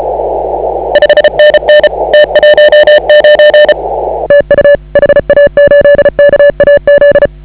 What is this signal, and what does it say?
Morse code, heard as a keyed tone, about 34 wpm, 560 Hz. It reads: 5NN T19 TU HA8KAZ
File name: AUDIO2019 / HA8KAZ CW